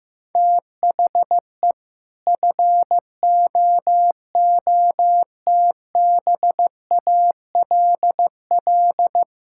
Morse code, heard as a keyed tone, about 15 wpm, 700 Hz. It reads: THE FOOTBALL
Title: Sentences from Top 1600 Words 15wpm